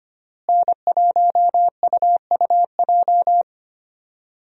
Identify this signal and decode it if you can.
Morse code, heard as a keyed tone, about 25 wpm, 700 Hz. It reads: N1UUJ
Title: USA Call Signs - 5 Characters - Repeated 3x - 25wpm